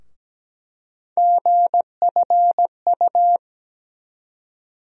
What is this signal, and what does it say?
Morse code, heard as a keyed tone, about 17 wpm, 700 Hz. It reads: GFU
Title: USA Call Sign Postfixes - 3 Letters 17wpm